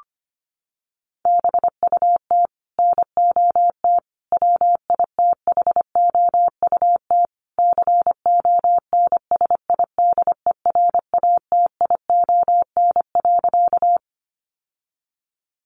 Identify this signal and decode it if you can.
Morse code, heard as a keyed tone, about 25 wpm, 700 Hz. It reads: BUT NOT WITHOUT CONSIDERATION.
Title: Sentences from Top 1500 Words 25wpm